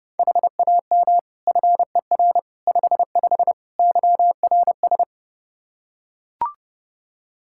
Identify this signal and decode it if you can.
Morse code, heard as a keyed tone, about 32 wpm, 700 Hz. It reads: HAM FER 55 YRS
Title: Practice QSO - Exchange 3 - More Info - 30wpm